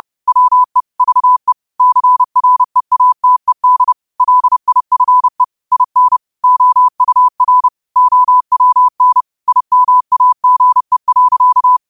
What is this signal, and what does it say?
Morse code, heard as a keyed tone, about 30 wpm, 1000 Hz. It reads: WE VE CREATED LIFE IN OUR OWN IMAGE.